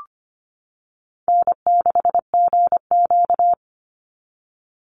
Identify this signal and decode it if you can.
Morse code, heard as a keyed tone, about 25 wpm, 700 Hz. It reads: N6GQ
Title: Easy Call Signs Repeated 3x - 25wpm